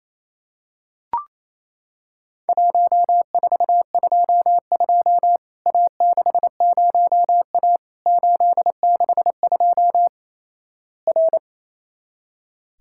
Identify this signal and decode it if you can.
Morse code, heard as a keyed tone, about 27 wpm, 700 Hz. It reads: 1422 A60A 862 R
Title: Worked All Europe DX Contest - QTC Exchange - 28wpm